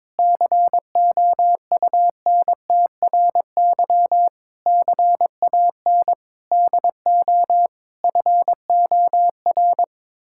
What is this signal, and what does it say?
Morse code, heard as a keyed tone, about 22 wpm, 700 Hz. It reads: COUNTRY CAN DO FOR
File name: practice / 22wpm / Sets of 8 Words - 22wpm